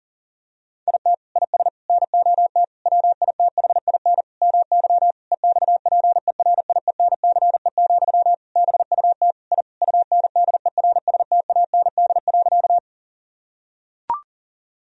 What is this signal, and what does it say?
Morse code, heard as a keyed tone, about 40 wpm, 700 Hz. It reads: IT IS NOT WITHIN MY EXPERIENCE, BUT I UNDERSTAND.
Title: Sentences from Top 1000 Words - Review Part 4 - 40wpm